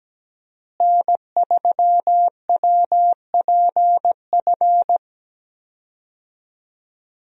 Morse code, heard as a keyed tone, about 17 wpm, 700 Hz. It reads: N3WPF
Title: USA Call Signs - 5 Characters - Repeated 3x - 17wpm